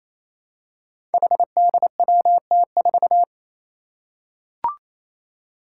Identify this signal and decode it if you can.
Morse code, heard as a keyed tone, about 28 wpm, 700 Hz. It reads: HDWT4
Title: Five Challenging Letter-Number Combinations 28wpm